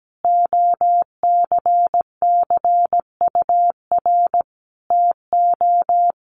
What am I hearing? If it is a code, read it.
Morse code, heard as a keyed tone, about 17 wpm, 700 Hz. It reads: OCCUR TO
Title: Sentences from Top 900 Words 17wpm